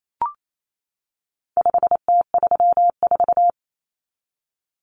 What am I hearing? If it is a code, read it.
Morse code, heard as a keyed tone, about 28 wpm, 700 Hz. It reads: 5T34
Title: Common Cut Numbers - 4-Digit 28wpm